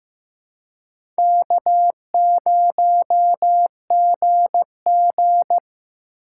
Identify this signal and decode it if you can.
Morse code, heard as a keyed tone, about 15 wpm, 700 Hz. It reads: K0GG
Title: USA Call Signs - 4 Characters - 4x Character Spacing - 15wpm